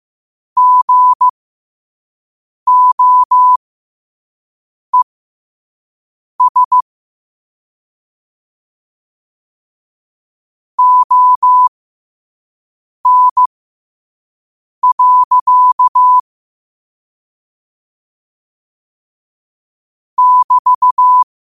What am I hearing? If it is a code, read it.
Morse code, heard as a keyed tone, about 15 wpm, spaced as for 5 wpm, 1000 Hz. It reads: GOES ON. <BT>
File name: qotdfiles / 5wpm